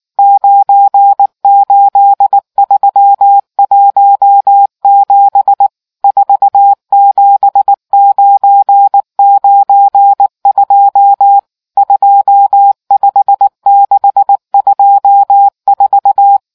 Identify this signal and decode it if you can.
Morse code, heard as a keyed tone, about 19 wpm, 800 Hz. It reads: 98317 47992 25624